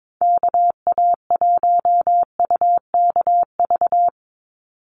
Morse code, heard as a keyed tone, about 22 wpm, 700 Hz. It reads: KA1UKV
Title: USA Call Signs 22wpm